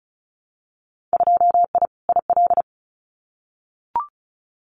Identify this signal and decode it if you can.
Morse code, heard as a keyed tone, about 34 wpm, 700 Hz. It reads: 2I IL
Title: Winter Field Day Exchanges 35wpm